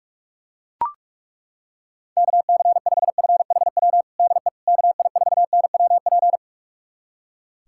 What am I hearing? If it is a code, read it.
Morse code, heard as a keyed tone, about 41 wpm, 700 Hz. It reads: KC5FHW DE KI4NWP
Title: USA Callsign DE Callsign 45wpm